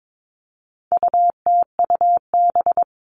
Morse code, heard as a keyed tone, about 22 wpm, 700 Hz. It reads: UTUB
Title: All Cut Numbers - 4-Digit 22wpm